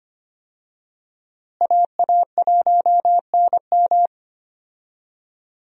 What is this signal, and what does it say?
Morse code, heard as a keyed tone, about 25 wpm, 700 Hz. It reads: AA1NM
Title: USA Call Signs - 5 Characters - Repeated 3x - 25wpm